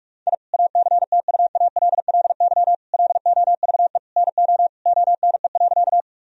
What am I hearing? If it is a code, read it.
Morse code, heard as a keyed tone, about 46 wpm, 700 Hz. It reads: I ACTUALLY LOVE NO ONE.
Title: Sentences from Top 1000 Words - Review Part 4 - 45wpm